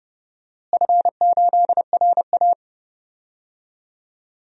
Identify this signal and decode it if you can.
Morse code, heard as a keyed tone, about 31 wpm, 700 Hz. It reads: F8RA
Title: DXCC-100 Call Signs - 4 Characters - Repeated 3x - 30wpm